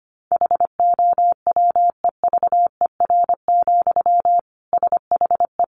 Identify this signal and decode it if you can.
Morse code, heard as a keyed tone, about 25 wpm, 700 Hz. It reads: HOWEVER, SHE
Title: Sentences from Top 600 Words 25wpm